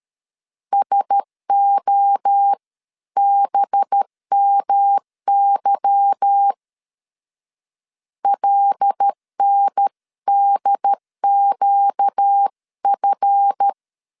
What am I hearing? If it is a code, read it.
Morse code, heard as a keyed tone, about 13 wpm, 800 Hz. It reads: SOBMY LNDQF
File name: CW / DET / 1 9 1500C S C 12wpm 3811